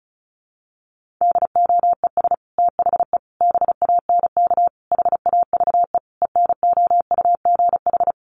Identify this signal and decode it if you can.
Morse code, heard as a keyed tone, about 34 wpm, 700 Hz. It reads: DOES THE BANK HAVE ENOUGH
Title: Sentences from Top 1500 Words 35wpm